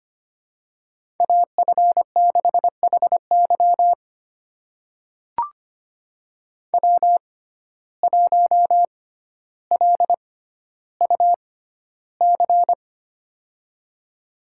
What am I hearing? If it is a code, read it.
Morse code, heard as a keyed tone, about 25 wpm, spaced as for 13 wpm, 700 Hz. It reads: AF6HY W1LUC
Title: USA Call Signs - 5 Characters - 6x Character Spacing - 25wpm